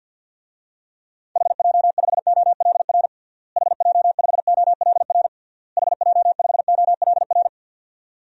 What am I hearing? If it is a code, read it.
Morse code, heard as a keyed tone, about 49 wpm, 700 Hz. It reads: HJ5OLR HJ5OLR HJ5OLR
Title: DXCC-100 Call Signs - 6 Characters - Repeated 3x - 50wpm